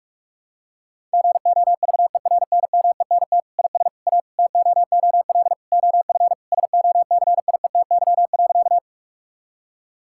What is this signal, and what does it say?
Morse code, heard as a keyed tone, about 46 wpm, 700 Hz. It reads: GOVERNMENT IS A TOOL OF SOCIETY.
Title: Sentences from Top 1000 Words - Review Part 6 - 45wpm